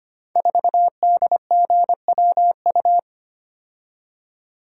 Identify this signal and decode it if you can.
Morse code, heard as a keyed tone, about 25 wpm, 700 Hz. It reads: 4DGWU
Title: Code Groups - 5 Letter:Numbers - 25wpm